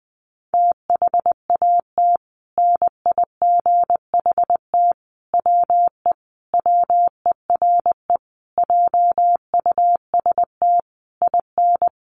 Morse code, heard as a keyed tone, about 20 wpm, 700 Hz. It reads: THAT NIGHT WE WERE JUST IN
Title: Sentences from Top 300 Words 20wpm